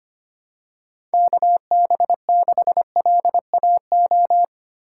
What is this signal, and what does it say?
Morse code, heard as a keyed tone, about 25 wpm, 700 Hz. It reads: KB6LAO
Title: USA Call Signs - 6 Characters - 6x Character Spacing - 25wpm